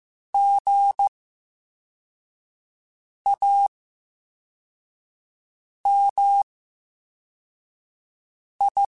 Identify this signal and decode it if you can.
Morse code, heard as a keyed tone, about 15 wpm, spaced as for 4 wpm, 800 Hz. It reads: GAMI
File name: Special INGs / Special INGs-mp3s / Slow ing @13